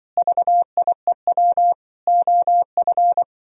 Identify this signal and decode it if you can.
Morse code, heard as a keyed tone, about 25 wpm, 700 Hz. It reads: VIEW OF
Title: Speed-Racing - Sets of 5 Words 20wpm